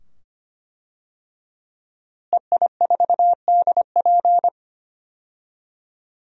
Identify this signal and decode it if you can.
Morse code, heard as a keyed tone, about 25 wpm, 700 Hz. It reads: EI4DP
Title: DXCC-100 Call Signs - 5 Characters - Repeated 2x - 25wpm